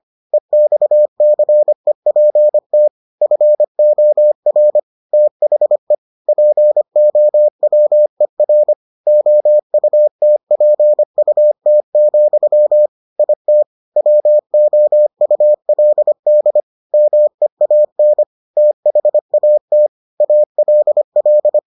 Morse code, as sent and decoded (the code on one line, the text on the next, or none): . -..- -.-. .--. - / ..-. --- .-. / - .... / .--. --- .-- . .-. / --- ..- - .--. ..- - --..-- / .. - / .-- --- ..- .-.. -.. / -- . .- -. / - .... .- - / .- .-.. .-..
EXCEPT FOR THE POWER OUTPUT, IT WOULD MEAN THAT ALL